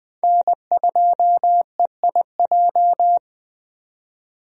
Morse code, heard as a keyed tone, about 20 wpm, 700 Hz. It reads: N2EIJ
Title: USA Call Signs - 5 Characters - 20wpm